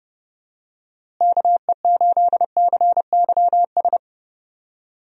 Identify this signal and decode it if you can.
Morse code, heard as a keyed tone, about 32 wpm, 700 Hz. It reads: KE8CYS
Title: USA Call Signs - 6 Characters - Repeated 2x - 30wpm